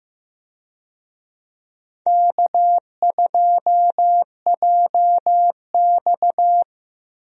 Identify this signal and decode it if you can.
Morse code, heard as a keyed tone, about 16 wpm, 700 Hz. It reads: K2JX
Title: USA Call Signs - 4 Characters - Repeated 2x - 15wpm